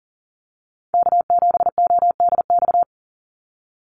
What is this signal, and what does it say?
Morse code, heard as a keyed tone, about 40 wpm, 700 Hz. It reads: K7ODX
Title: NA SOTA Activator Callsigns 40wpm